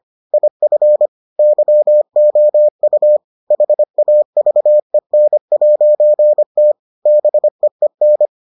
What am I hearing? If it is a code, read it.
Morse code, heard as a keyed tone, about 25 wpm, 600 Hz. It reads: IF YOU HAVEN'T BEEN